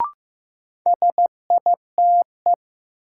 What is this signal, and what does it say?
Morse code, heard as a keyed tone, about 15 wpm, 700 Hz. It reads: SITE
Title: Sentences from Top 1000 Words - Review Part 1 - New Word Drill - 15wpm